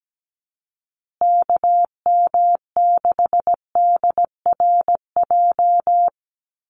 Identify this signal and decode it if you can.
Morse code, heard as a keyed tone, about 17 wpm, 700 Hz. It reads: KM6DRJ
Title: USA Call Signs Repeated 3x 17wpm